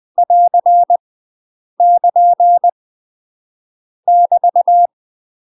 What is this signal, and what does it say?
Morse code, heard as a keyed tone, about 20 wpm, 700 Hz. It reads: <AR> <KN> <BT>
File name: UK MASTER QSOs TRAINING / 20WPM / UK MASTER QSOs 0036